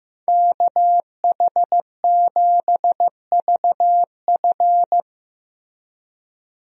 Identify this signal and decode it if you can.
Morse code, heard as a keyed tone, about 16 wpm, 700 Hz. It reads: KH7VF
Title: USA Call Signs - 5 Characters - Repeated 2x - 15wpm